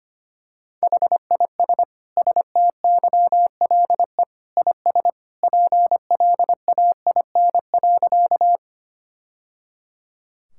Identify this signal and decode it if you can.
Morse code, heard as a keyed tone, about 25 wpm, 700 Hz. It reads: HIS STYLE IS PLAIN.
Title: Sentences from Top 1000 Words - Review Part 5 - 25wpm